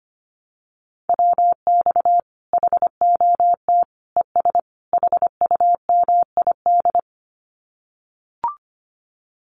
Morse code, heard as a keyed tone, about 25 wpm, 700 Hz. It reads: WX HOT ES HUMID
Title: Common QSO Phrases 25wpm